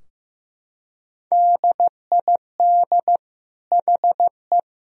Morse code, heard as a keyed tone, about 16 wpm, 700 Hz. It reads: DID HE
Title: Sentences from Top 1000 Words - Review Part 3 - 15wpm